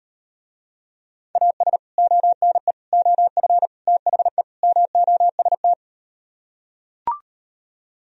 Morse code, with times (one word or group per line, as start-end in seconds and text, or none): 1.4-1.8 AS
2.0-2.7 ONE
2.9-3.7 OF
3.9-4.4 THE
4.6-5.7 MOST